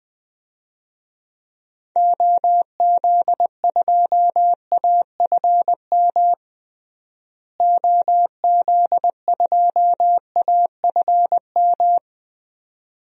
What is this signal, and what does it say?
Morse code, heard as a keyed tone, about 20 wpm, 700 Hz. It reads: OZ2AFM OZ2AFM
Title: DXCC-100 Call Signs - 6 Characters - Repeated 2x - 20wpm